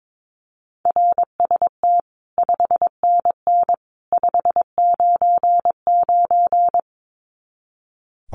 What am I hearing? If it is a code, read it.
Morse code, heard as a keyed tone, about 22 wpm, 700 Hz. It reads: RST 5NN 599
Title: Common QSO Phrases 22wpm